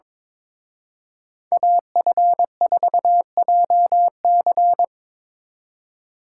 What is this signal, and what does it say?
Morse code, heard as a keyed tone, about 25 wpm, 700 Hz. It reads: AF4JC
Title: Speed Racing - USA 2x2 Call Signs 22wpm